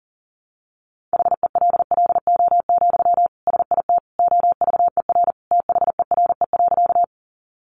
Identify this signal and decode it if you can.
Morse code, heard as a keyed tone, about 40 wpm, 700 Hz. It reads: HELLO, SIT OVER THERE.
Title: Sentences from Top 1100 Words 40wpm